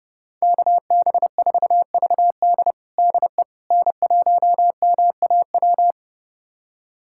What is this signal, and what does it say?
Morse code, heard as a keyed tone, about 32 wpm, 700 Hz. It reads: KB4VD DE N1MAW
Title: USA Callsign DE Callsign - 5 Characters - 30wpm